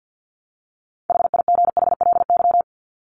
Morse code, heard as a keyed tone, about 50 wpm, 700 Hz. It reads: HIGHLY
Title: LY Suffix - Top 100 Words with Base Words 50wpm